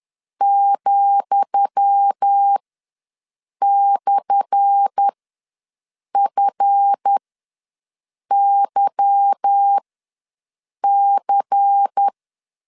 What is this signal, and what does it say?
Morse code, heard as a keyed tone, about 10 wpm, spaced as for 6 wpm, 800 Hz. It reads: ,/FYC
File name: LEZIONI / 10 SIGNS / 10D COMA ws 10wpm 4314